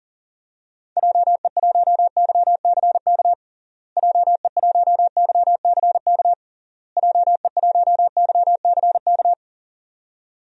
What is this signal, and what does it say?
Morse code, heard as a keyed tone, about 37 wpm, 700 Hz. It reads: JE1YCK JE1YCK JE1YCK
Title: DXCC-100 Call Signs - 6 Characters - Repeated 3x - 40wpm